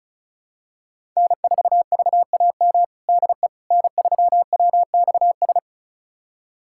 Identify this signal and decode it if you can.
Morse code, heard as a keyed tone, about 34 wpm, 700 Hz. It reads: N4VAM DE N3WXS
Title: USA Callsign DE Callsign - 5 Characters - 35wpm